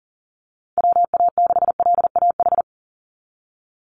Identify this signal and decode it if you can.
Morse code, heard as a keyed tone, about 40 wpm, 700 Hz. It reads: WA6LAH
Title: USA Call Signs 40wpm